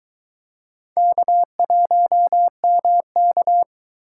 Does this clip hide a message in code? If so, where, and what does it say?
Yes: Morse code from 1.0 to 3.6 s, heard as a keyed tone, about 22 wpm, 700 Hz. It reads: K1MK